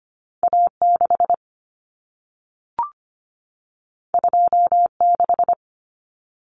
Morse code, heard as a keyed tone, about 25 wpm, 700 Hz. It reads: A6 26
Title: Common Cut Numbers - 2-Digit 25wpm